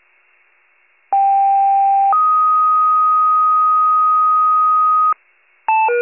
Two-tone page, heard as a one-second tone, then a three-second tone, about 790 then 1250 Hz. It is heard from 1.1 to 5.1 s.
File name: wcalert